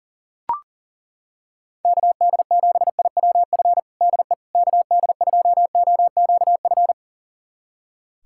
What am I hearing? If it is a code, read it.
Morse code, heard as a keyed tone, about 40 wpm, 700 Hz. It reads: KD7IWF DE KD2OQF
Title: USA Callsign DE Callsign - 6 Characters - 40wpm